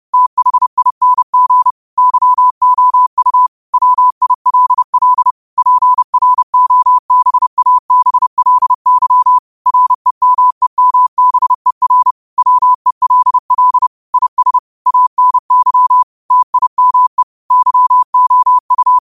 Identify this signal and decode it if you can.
Morse code, heard as a keyed tone, about 30 wpm, 1000 Hz. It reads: THING YOU WILL PROBABLY REMEMBER WELL IS ANY TIME YOU
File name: qotdfiles / 30wpm